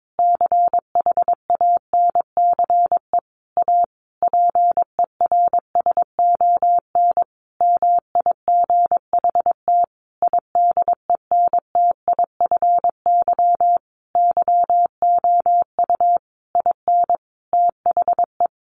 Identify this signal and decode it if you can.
Morse code, heard as a keyed tone, about 22 wpm, 700 Hz. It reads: CHANCE A PERSON MIGHT IDENTIFY YOU IN THE
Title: Sentences from Top 900 Words 22wpm